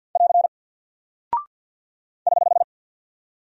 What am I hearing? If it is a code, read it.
Morse code, heard as a keyed tone, about 50 wpm, 700 Hz. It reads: <AR> <HH>
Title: Single Character (Letter:Number:Punctuation:Prosign) 50wpm